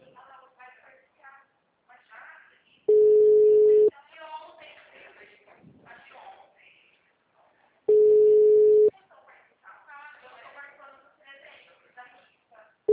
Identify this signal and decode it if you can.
Ringback tone (Europe, 425 Hz), with speech.
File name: out- -2010- - - .20817